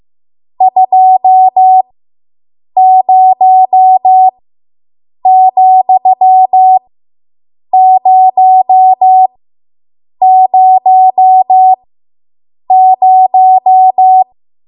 Morse code, heard as a keyed tone, about 15 wpm, spaced as for 7 wpm, 750 Hz. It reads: 20,000